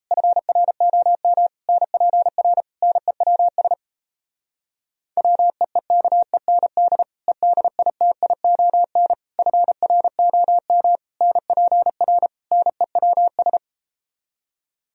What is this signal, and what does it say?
Morse code, heard as a keyed tone, about 37 wpm, 700 Hz. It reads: FROM NPR NEWS WEEKEND EDITION FROM NPR NEWS